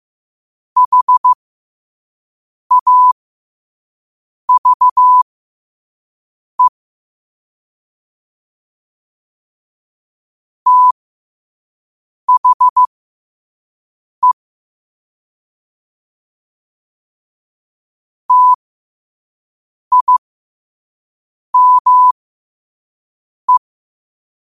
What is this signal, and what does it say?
Morse code, heard as a keyed tone, about 15 wpm, spaced as for 5 wpm, 1000 Hz. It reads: HAVE THE TIME